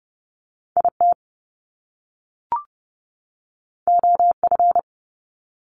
Morse code, heard as a keyed tone, about 30 wpm, 700 Hz. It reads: IT OF
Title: Top Two Letter Combinations in Words 30wpm